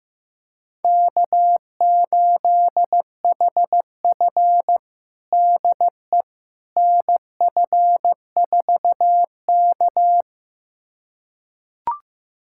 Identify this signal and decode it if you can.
Morse code, heard as a keyed tone, about 15 wpm, 700 Hz. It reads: K8HF DE NF4K
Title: USA Callsign DE Callsign - 4 Characters - 15wpm